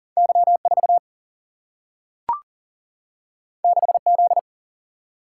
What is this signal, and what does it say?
Morse code, heard as a keyed tone, about 39 wpm, 700 Hz. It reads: Y4 6Z
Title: Single-Letter Single-Number 40wpm